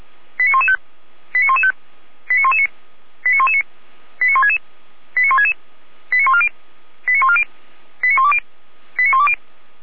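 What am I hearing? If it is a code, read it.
Five-tone selective call (ZVEI), heard as quick runs of five tones, tones of about 70 ms. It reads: 89115 89115 89119 89119 89166 89166 89140 89140 89120 89120
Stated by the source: Five-tone ZVEI selective call 89073 received on SDR by feuerwehr-orlamuende fire brigade.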